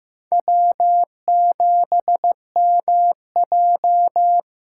Morse code, heard as a keyed tone, about 15 wpm, 700 Hz. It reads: W7MJ
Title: USA Call Signs - 5 Characters - 6x Character Spacing - 15wpm